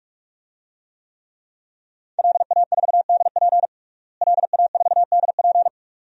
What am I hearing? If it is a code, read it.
Morse code, heard as a keyed tone, about 48 wpm, 700 Hz. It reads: LA4DP LA4DP
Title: DXCC-100 Call Signs - 5 Characters - Repeated 2x - 45wpm